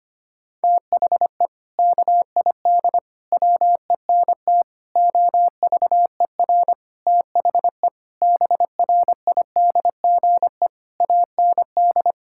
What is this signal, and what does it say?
Morse code, heard as a keyed tone, about 25 wpm, 700 Hz. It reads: THE KID WENT OVER THE BRIDGE AND
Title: Sentences from Top 1600 Words 25wpm